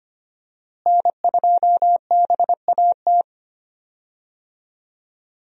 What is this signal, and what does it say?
Morse code, heard as a keyed tone, about 25 wpm, 700 Hz. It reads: N2BAT
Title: USA Call Signs - 5 Characters - Repeated 2x - 25wpm